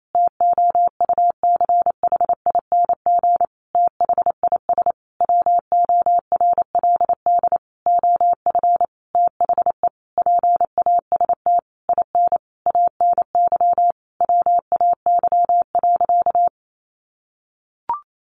Morse code, heard as a keyed tone, about 28 wpm, 700 Hz. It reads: TOUCHING THIS WORLD OF THE PAST IN ANY WAY.
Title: A Sound Of Thunder 28wpm